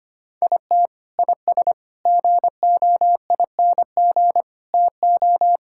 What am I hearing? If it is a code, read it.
Morse code, heard as a keyed tone, about 25 wpm, 700 Hz. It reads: IT IS GOING TO
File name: practice / 25wpm / Sentences with 4-Word Phrases - 25wpm